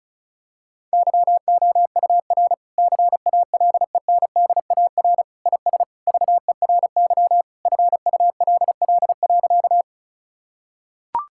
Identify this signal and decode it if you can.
Morse code, heard as a keyed tone, about 34 wpm, 700 Hz. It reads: YOUR CALENDAR IS VERY FULL.
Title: Sentences from Top 1600 Words 35wpm